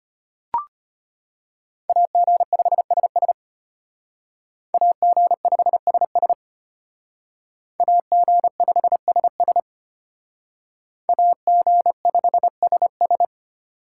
Morse code, heard as a keyed tone, about 33 wpm, 700 Hz. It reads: AG5SS AG5SS AG5SS AG5SS